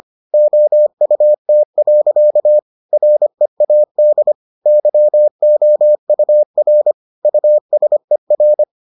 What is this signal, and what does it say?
Morse code, heard as a keyed tone, about 25 wpm, 600 Hz. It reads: OUT. READ YOUR USER